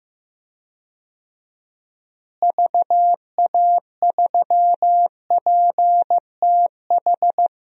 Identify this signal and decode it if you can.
Morse code, heard as a keyed tone, about 15 wpm, 700 Hz. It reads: VA3PTH
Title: DXCC-100 Call Signs - 6 Characters - Repeated 2x - 15wpm